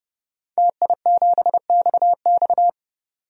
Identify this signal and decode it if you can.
Morse code, heard as a keyed tone, about 30 wpm, 700 Hz. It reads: TI7XX
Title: DXCC-100 Call Signs - 5 Characters - 30wpm